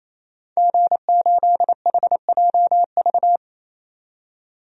Morse code, heard as a keyed tone, about 28 wpm, 700 Hz. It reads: G8HJV
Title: DXCC-100 Call Signs - 5 Characters - 28wpm